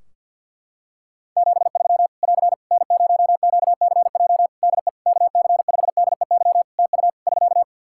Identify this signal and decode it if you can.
Morse code, heard as a keyed tone, about 49 wpm, 700 Hz. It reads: 73 <AR> N0QCJ DE KC5DEY TU <SK>
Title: Practice QSO - Exchange 4 - Ending - 50wpm